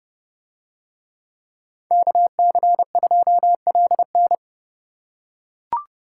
Morse code, heard as a keyed tone, about 30 wpm, 700 Hz. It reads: KC2LN